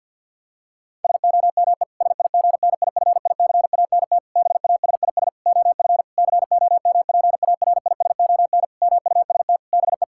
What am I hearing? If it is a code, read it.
Morse code, heard as a keyed tone, about 52 wpm, 700 Hz. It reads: SOME SIGNIFICANT BASIS OF COMPARISON MUST BE
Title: Sentences from Top 1600 Words 50wpm